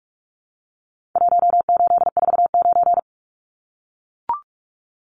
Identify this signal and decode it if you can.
Morse code, heard as a keyed tone, about 44 wpm, 700 Hz. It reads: 1849
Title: Numbers - Contest Style - 4-Digit 45wpm